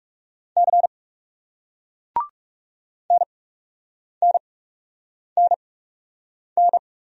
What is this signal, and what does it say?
Morse code, heard as a keyed tone, about 40 wpm, spaced as for 8 wpm, 700 Hz. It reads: C NNNN